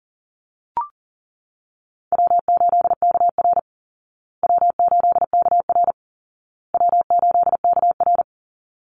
Morse code, heard as a keyed tone, about 40 wpm, 700 Hz. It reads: W8KR W8KR W8KR
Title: Easy Call Signs Repeated 3x - 40wpm